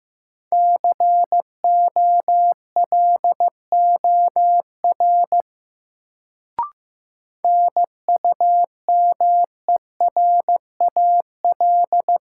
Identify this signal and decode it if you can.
Morse code, heard as a keyed tone, about 15 wpm, 700 Hz. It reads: COLOR NUMERAL
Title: Sentences from Top 1000 Words - Review Part 2 - New Word Drill - 15wpm